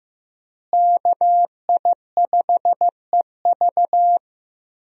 Morse code, heard as a keyed tone, about 15 wpm, 700 Hz. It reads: KI5EV
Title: USA Call Signs - 5 Characters - Repeated 3x - 15wpm